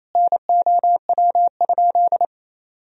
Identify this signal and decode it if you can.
Morse code, heard as a keyed tone, about 28 wpm, 700 Hz. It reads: NOW?
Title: Sentences from Top 200 Words and Abbreviations 28wpm